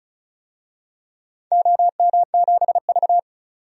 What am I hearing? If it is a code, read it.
Morse code, heard as a keyed tone, about 36 wpm, 700 Hz. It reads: OM7V
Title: DXCC-100 Call Signs - 4 Characters - 35wpm